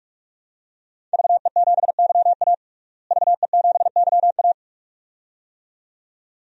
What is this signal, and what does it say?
Morse code, heard as a keyed tone, about 41 wpm, 700 Hz. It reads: VE7YA VE7YA
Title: DXCC-100 Call Signs - 5 Characters - Repeated 2x - 45wpm